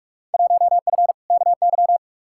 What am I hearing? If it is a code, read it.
Morse code, heard as a keyed tone, about 42 wpm, 700 Hz. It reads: 1F KY
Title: Field Day Exchanges Encore - 45wpm